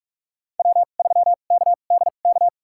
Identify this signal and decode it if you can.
Morse code, heard as a keyed tone, about 42 wpm, spaced as for 31 wpm, 700 Hz. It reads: W3KDK
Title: USA Call Signs - 5 Characters - 2x Character Spacing - 45wpm